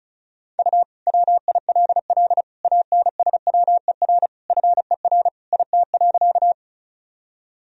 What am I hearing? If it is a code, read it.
Morse code, heard as a keyed tone, about 34 wpm, 700 Hz. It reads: U WILL ANSWER FER IT.
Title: Sentences from Top 300 Words and Abbreviations 35wpm